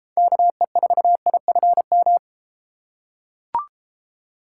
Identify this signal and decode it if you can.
Morse code, heard as a keyed tone, about 31 wpm, 700 Hz. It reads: KE4IFM